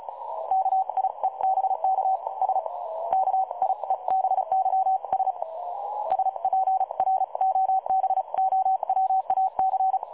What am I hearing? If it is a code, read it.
Morse code, heard as a keyed tone, about 35 wpm, spaced as for 31 wpm, 760 Hz. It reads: KIEBK S KIEBYS SPROBOWAC
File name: dlazaawansowanych / nagraniazpasma / websdr recording 3565.0kHz